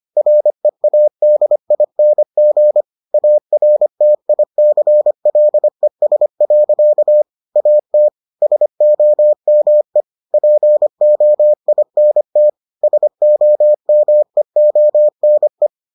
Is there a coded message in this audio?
Yes: Morse code from 0.2 to 15.7 s, heard as a keyed tone, about 25 wpm, 600 Hz. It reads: READING ARTICLES. AT SOME POINT SOMEONE